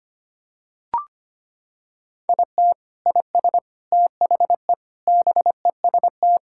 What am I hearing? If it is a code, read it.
Morse code, heard as a keyed tone, about 25 wpm, 700 Hz. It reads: IT IS THE BEST
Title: Sentences from Top 1000 Words - Review Part 6 - 25wpm